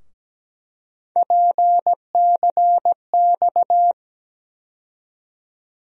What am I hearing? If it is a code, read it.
Morse code, heard as a keyed tone, about 17 wpm, 700 Hz. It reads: PCX
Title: USA Call Sign Postfixes - 3 Letters 17wpm